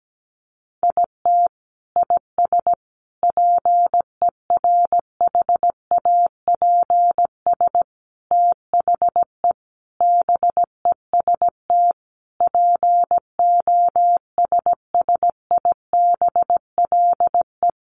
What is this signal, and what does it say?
Morse code, heard as a keyed tone, about 17 wpm, 700 Hz. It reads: IT IS PERHAPS THE BEST POSSIBLE
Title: Sentences from Top 1300 Words 17wpm